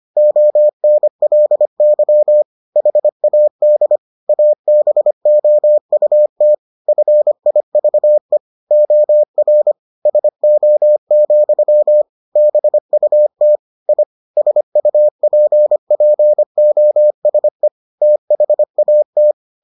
Morse code, heard as a keyed tone, about 25 wpm, 600 Hz. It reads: ONLY HAD ABOUT FIVE OR SO, BUT I SUPPOSE THAT